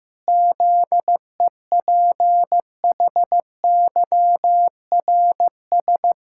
Morse code, heard as a keyed tone, about 15 wpm, 700 Hz. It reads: ZEPHYRS
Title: Pangram Challenge v2 - 15wpm